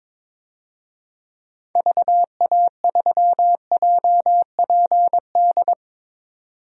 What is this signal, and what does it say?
Morse code, heard as a keyed tone, about 22 wpm, 700 Hz. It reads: VA3JPD
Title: DXCC-100 Call Signs - 6 Characters - 22wpm